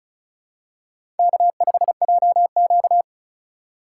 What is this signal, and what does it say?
Morse code, heard as a keyed tone, about 36 wpm, 690 Hz. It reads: K5JQ
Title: USA Call Signs - 4 Characters - 2x Character Spacing - 35wpm